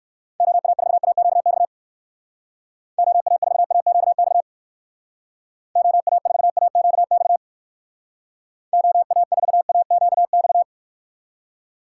Morse code, heard as a keyed tone, about 64 wpm, 700 Hz. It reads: OA4AQX OA4AQX OA4AQX OA4AQX